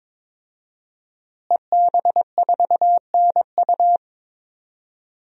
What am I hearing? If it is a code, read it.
Morse code, heard as a keyed tone, about 22 wpm, 700 Hz. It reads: EB4NU
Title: DXCC-100 Call Signs - 5 Characters - Repeated 2x - 22wpm